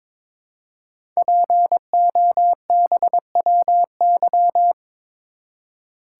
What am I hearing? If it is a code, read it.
Morse code, heard as a keyed tone, about 22 wpm, 700 Hz. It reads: POBWY
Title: Code Groups - 5 Letters - 22wpm